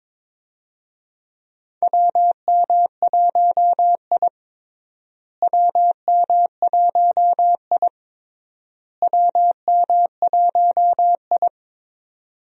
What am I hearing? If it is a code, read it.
Morse code, heard as a keyed tone, about 21 wpm, 700 Hz. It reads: WM1I WM1I WM1I
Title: USA Call Signs - 4 Characters - Repeated 3x - 22wpm